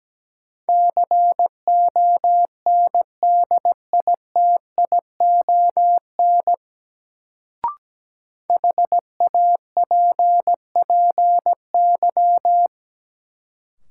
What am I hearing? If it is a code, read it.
Morse code, heard as a keyed tone, about 17 wpm, 700 Hz. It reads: CONDITION HAPPY
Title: Sentences from Top 1000 Words - New Word Drill - 17wpm